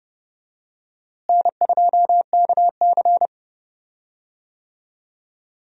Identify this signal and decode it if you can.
Morse code, heard as a keyed tone, about 31 wpm, 700 Hz. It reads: N2KC
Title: USA Call Signs - 4 Characters - Repeated 3x - 30wpm